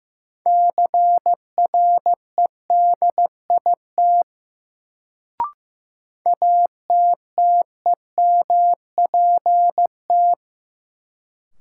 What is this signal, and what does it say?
Morse code, heard as a keyed tone, about 15 wpm, 700 Hz. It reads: CREDIT ATTEMPT
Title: Sentences from Top 1000 Words - New Word Drill - 15wpm